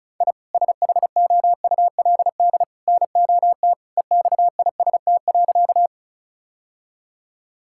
Morse code, heard as a keyed tone, about 36 wpm, 700 Hz. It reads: I SHOULD NOT EXIST.
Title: Sentences from Top 1000 Words - Review Part 5 - 35wpm